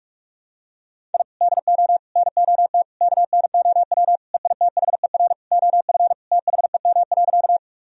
Morse code, heard as a keyed tone, about 46 wpm, 700 Hz. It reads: I DO NOT KNOW EITHER OF THEM.
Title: Sentences from Top 1000 Words - Review Part 4 - 45wpm